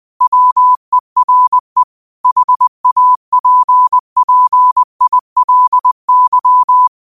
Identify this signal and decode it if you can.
Morse code, heard as a keyed tone, about 20 wpm, 1000 Hz. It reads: WERE HAPPILY
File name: qotdfiles / 20wpm